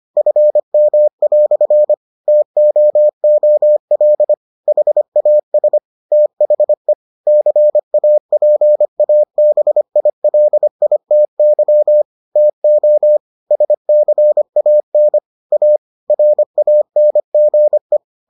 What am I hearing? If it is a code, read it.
Morse code, heard as a keyed tone, about 25 wpm, 600 Hz. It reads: FM" TOOL HAS THE CAPABILITY TO SCAN A RANGE